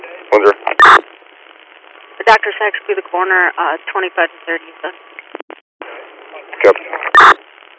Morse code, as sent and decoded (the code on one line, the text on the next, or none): none